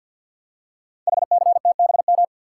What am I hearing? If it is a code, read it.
Morse code, heard as a keyed tone, about 50 wpm, 690 Hz. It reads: HCTBM
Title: Code Groups - 5 Letters - 50wpm